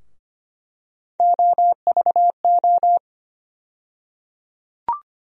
Morse code, heard as a keyed tone, about 25 wpm, 700 Hz. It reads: OVO
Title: USA Call Sign Postfixes - 3 Letters 25wpm